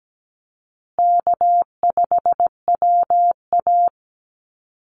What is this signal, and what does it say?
Morse code, heard as a keyed tone, about 17 wpm, 700 Hz. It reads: K5WA
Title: Call Signs Repeated 2x - 17wpm